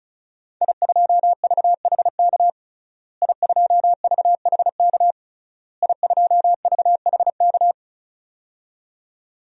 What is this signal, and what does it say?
Morse code, heard as a keyed tone, about 36 wpm, 700 Hz. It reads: I2VHK I2VHK I2VHK
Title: DXCC-100 Call Signs - 5 Characters - Repeated 3x - 35wpm